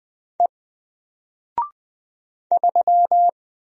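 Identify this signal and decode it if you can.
Morse code, heard as a keyed tone, about 20 wpm, 700 Hz. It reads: E 3
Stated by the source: Single letter and number Morse drill at 20 wpm, from N4LCD.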